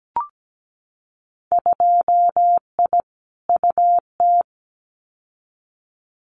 Morse code, heard as a keyed tone, about 17 wpm, 700 Hz. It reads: 2I UT
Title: Winter Field Day Exchanges 17wpm